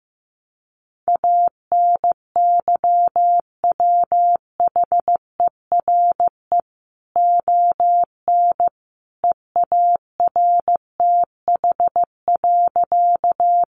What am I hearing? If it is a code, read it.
Morse code, heard as a keyed tone, about 15 wpm, 700 Hz. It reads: ANYWHERE ON EARTH.
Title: Sentences from Top 1400 Words 15wpm